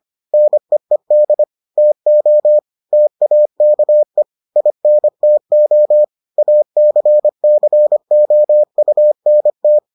Morse code, heard as a keyed tone, about 25 wpm, 600 Hz. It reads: NEED TO TAKE INTO ACCOUNT